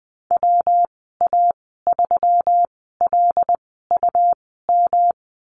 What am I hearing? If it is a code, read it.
Morse code, heard as a keyed tone, about 20 wpm, spaced as for 14 wpm, 700 Hz. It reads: WA3LUM
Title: 154 Lesson 25g - Letter p - Call Signs 20wpm